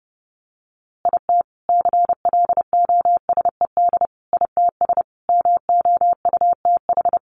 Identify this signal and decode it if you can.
Morse code, heard as a keyed tone, about 30 wpm, 700 Hz. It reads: IT CLOSED ITS MOUTH
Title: A Sound Of Thunder 30wpm